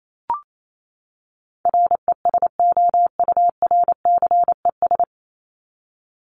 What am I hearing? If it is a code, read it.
Morse code, heard as a keyed tone, about 28 wpm, 700 Hz. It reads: RESOURCES
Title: ES:S Suffix - Top 100 Words with Base Words 28wpm